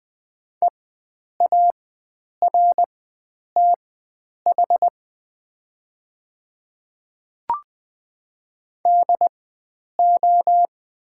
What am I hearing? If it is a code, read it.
Morse code, heard as a keyed tone, about 20 wpm, spaced as for 9 wpm, 700 Hz. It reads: EARTH DO
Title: 4x Character Spacing - Top 500 Words - 20wpm